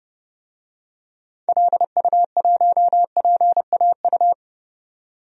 Morse code, heard as a keyed tone, about 30 wpm, 700 Hz. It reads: LU1PAU
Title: DXCC-100 Call Signs - 6 Characters - 30wpm